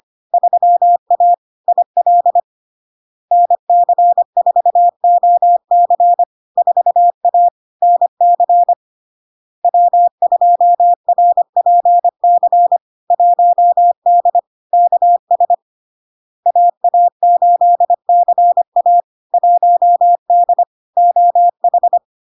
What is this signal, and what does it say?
Morse code, heard as a keyed tone, about 25 wpm, 700 Hz. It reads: 3A IL NC4OC 4A NC W2RPC 1D KS AA8CA 1D OH
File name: LARK-FD-2022-Log-in-CW-25wpm-700-Hz